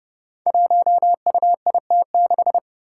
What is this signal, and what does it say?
Morse code, heard as a keyed tone, about 32 wpm, 700 Hz. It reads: 1UIT6
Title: Five Challenging Letter-Number Combinations 30wpm